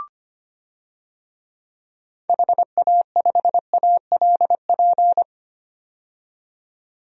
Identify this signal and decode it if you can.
Morse code, heard as a keyed tone, about 25 wpm, 700 Hz. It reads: HA5ALP
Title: DXCC-100 Call Signs - 6 Characters - 25wpm